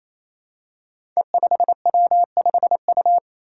Morse code, heard as a keyed tone, about 28 wpm, 700 Hz. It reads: E5W5U
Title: Five Challenging Letter-Number Combinations 28wpm